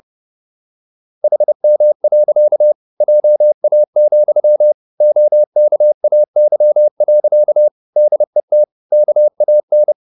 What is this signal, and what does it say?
Morse code, heard as a keyed tone, about 30 wpm, 600 Hz. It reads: HM. JA, OKAY. DET KAN
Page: Ferie 30 wpm